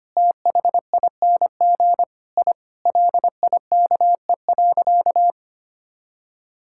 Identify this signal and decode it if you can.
Morse code, heard as a keyed tone, about 25 wpm, 700 Hz. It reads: THING I LIKE.
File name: practice / 25wpm / Sentences with 3-Word Phrases - 25wpm